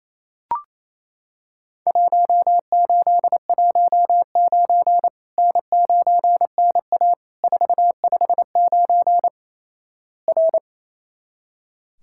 Morse code, heard as a keyed tone, about 28 wpm, 700 Hz. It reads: 1819 N9NA 459 R
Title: Worked All Europe DX Contest - QTC Exchange - 28wpm